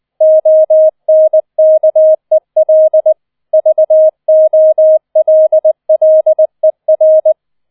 Morse code, heard as a keyed tone, about 20 wpm, 630 Hz. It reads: ONKEL VOLLER
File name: 20wpm